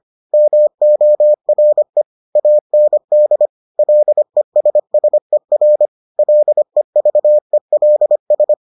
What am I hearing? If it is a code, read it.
Morse code, heard as a keyed tone, about 25 wpm, 600 Hz. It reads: MORE AND LESSER LEVELS